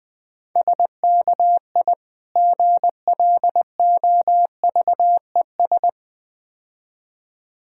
Morse code, heard as a keyed tone, about 22 wpm, 700 Hz. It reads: SKI GLOVES
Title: Difficult Words - Speed-Racing with Hints - 20wpm